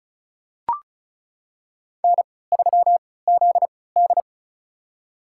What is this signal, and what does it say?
Morse code, heard as a keyed tone, about 36 wpm, spaced as for 20 wpm, 700 Hz. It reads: N3ZD
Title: USA Call Signs - 3x Character Spacing - 35wpm